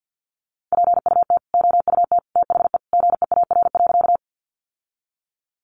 Morse code, heard as a keyed tone, about 50 wpm, 700 Hz. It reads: PUT OUT THE GEAR.
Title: Sentences from Top 1400 Words 50wpm